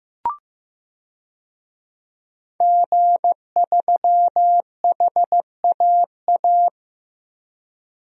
Morse code, heard as a keyed tone, about 15 wpm, 700 Hz. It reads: G3HAA
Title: DXCC-100 Call Signs - 5 Characters - Repeated 2x - 15wpm